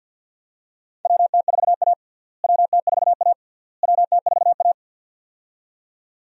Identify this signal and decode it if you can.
Morse code, heard as a keyed tone, about 47 wpm, 700 Hz. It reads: WT4A WT4A WT4A
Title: USA Call Signs - 4 Characters - Repeated 3x - 50wpm